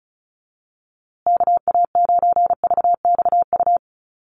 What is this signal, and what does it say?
Morse code, heard as a keyed tone, about 34 wpm, 700 Hz. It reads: KA9VXU
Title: USA Call Signs Repeated 2x 35wpm